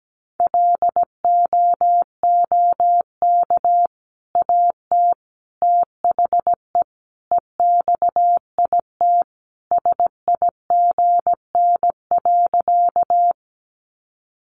Morse code, heard as a keyed tone, about 17 wpm, 700 Hz. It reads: LOOK AT THE EXIT SIGN.
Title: Sentences from Top 1100 Words 17wpm